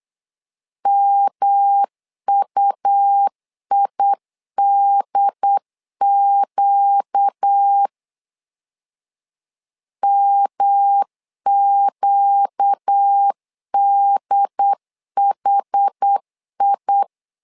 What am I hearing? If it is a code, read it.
Morse code, heard as a keyed tone, about 8 wpm, 800 Hz. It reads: MUIDQ MQDHI
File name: 4 Q 8wpm